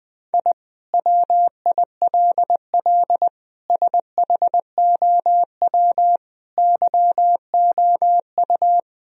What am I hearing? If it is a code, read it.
Morse code, heard as a keyed tone, about 21 wpm, 700 Hz. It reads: I WILL SHOW YOU
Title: Sentences from Top 1000 Words - Review Part 1 - 20wpm